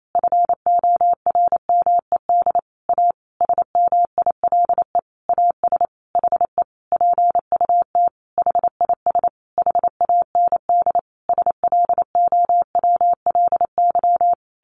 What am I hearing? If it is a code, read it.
Morse code, heard as a keyed tone, about 28 wpm, 700 Hz. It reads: FORMED A SMILE AS HE PUT HIS HAND SLOWLY
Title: A Sound Of Thunder 28wpm